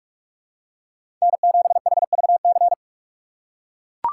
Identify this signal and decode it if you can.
Morse code, heard as a keyed tone, about 43 wpm, 700 Hz. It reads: N7HVC
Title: USA Call Signs - 2x Character Spacing - 45wpm